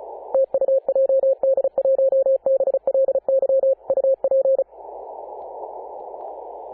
Morse code, heard as a keyed tone, about 34 wpm, 540 Hz. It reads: TU JD1BLY UP
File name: Apr/29 2042z JD1BLY 14.011MHz CW